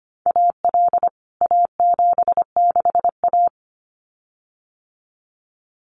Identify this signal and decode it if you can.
Morse code, heard as a keyed tone, about 25 wpm, 700 Hz. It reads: AL A76A
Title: CWT Exchange 25wpm